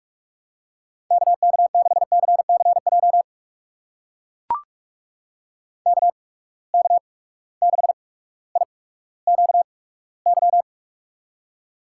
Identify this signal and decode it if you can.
Morse code, heard as a keyed tone, about 44 wpm, spaced as for 19 wpm, 700 Hz. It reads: KK6CCJ KK6IQY
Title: USA Call Signs - 6 Characters - 8x Character Spacing - 45wpm